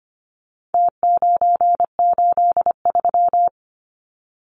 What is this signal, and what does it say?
Morse code, heard as a keyed tone, about 25 wpm, 700 Hz. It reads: T983
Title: Numbers - Contest Style - 4-Digit 25wpm